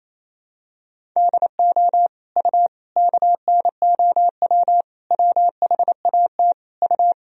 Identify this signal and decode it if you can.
Morse code, heard as a keyed tone, about 28 wpm, 700 Hz. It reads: DO U KNOW WHAT U
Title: Sentences from Top 100 Words and Abbreviations 28wpm